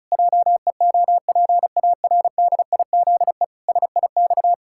Morse code, heard as a keyed tone, about 34 wpm, 700 Hz. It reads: JEOPARDIZE SIX
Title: Pangram Challenge v2 - 35wpm